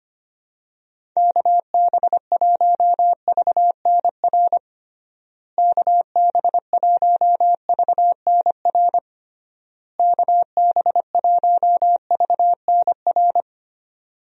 Morse code, heard as a keyed tone, about 25 wpm, 700 Hz. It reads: KB1VNR KB1VNR KB1VNR
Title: USA Call Signs - 6 Characters - Repeated 3x - 25wpm